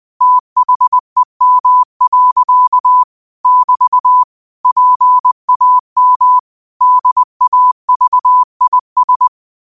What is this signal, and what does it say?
Morse code, heard as a keyed tone, about 20 wpm, 1000 Hz. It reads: THEM. <BT> PAM DAVIS
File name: qotdfiles / 20wpm